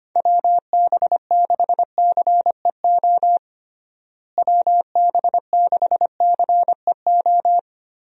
Morse code, heard as a keyed tone, about 25 wpm, 700 Hz. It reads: WB6CEO WB6CEO
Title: USA Call Signs - 6 Characters - Repeated 3x - 25wpm